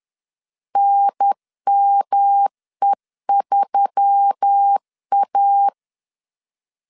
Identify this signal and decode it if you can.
Morse code, heard as a keyed tone, about 10 wpm, 800 Hz. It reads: NME3A
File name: LEZIONI / 6 PKX0 / 6 PKX 10wpm 2426